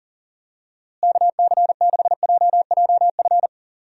Morse code, heard as a keyed tone, about 40 wpm, 700 Hz. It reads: KC6JJF
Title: USA Call Signs - 6 Characters - Repeated 2x - 40wpm